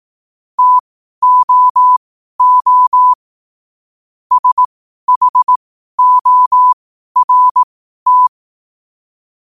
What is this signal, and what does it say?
Morse code, heard as a keyed tone, about 18 wpm, spaced as for 12 wpm, 1000 Hz. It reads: TOO SHORT